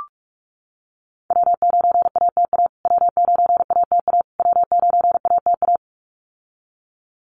Morse code, heard as a keyed tone, about 44 wpm, 700 Hz. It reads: W9ATA W9ATA W9ATA
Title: USA Call Signs Repeated 3x 45wpm